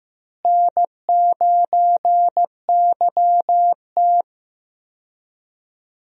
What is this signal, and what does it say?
Morse code, heard as a keyed tone, about 15 wpm, 700 Hz. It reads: N9YT
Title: USA Call Signs - 4 Characters - Repeated 2x - 15wpm